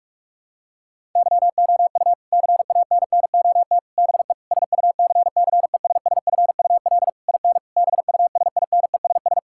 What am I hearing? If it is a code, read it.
Morse code, heard as a keyed tone, about 46 wpm, 700 Hz. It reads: YOU CANNOT BE SUCCESSFUL IN BUSINESS
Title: Sentences from Top 1000 Words - Review Part 6 - 45wpm